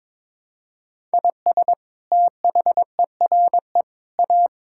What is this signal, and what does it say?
Morse code, heard as a keyed tone, about 21 wpm, 700 Hz. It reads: IS THERE A
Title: Sentences from Top 1000 Words - Review Part 5 - 22wpm